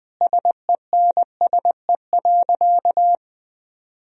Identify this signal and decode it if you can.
Morse code, heard as a keyed tone, about 20 wpm, 700 Hz. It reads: SENSE.
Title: Sentences from Top 1000 Words - Review Part 4 - 20wpm